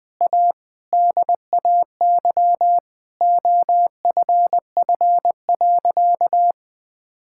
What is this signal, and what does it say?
Morse code, heard as a keyed tone, about 20 wpm, 700 Hz. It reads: A DAY OFF.
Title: Sentences with 7-Word Phrases - 20wpm